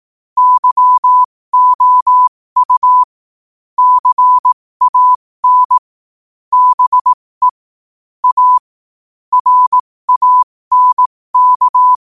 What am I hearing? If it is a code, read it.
Morse code, heard as a keyed tone, about 18 wpm, spaced as for 15 wpm, 1000 Hz. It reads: YOU CAN BE A RANK
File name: qotdfiles / 15wpm